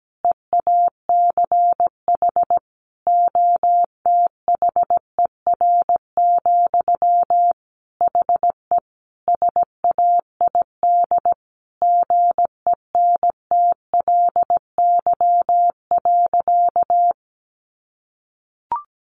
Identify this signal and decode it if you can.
Morse code, heard as a keyed tone, about 17 wpm, 700 Hz. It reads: EACH OTHER, HE SAID GENTLY.
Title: Sentences from Top 1500 Words 17wpm